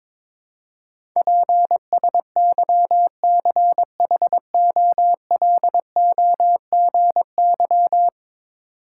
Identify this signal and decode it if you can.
Morse code, heard as a keyed tone, about 22 wpm, 700 Hz. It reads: PSYCHOLOGY
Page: morse-code-ninja-practice-22wpm